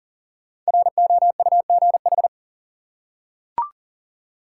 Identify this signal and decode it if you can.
Morse code, heard as a keyed tone, about 40 wpm, 700 Hz. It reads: ROUGH
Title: Five Letter Words - 40wpm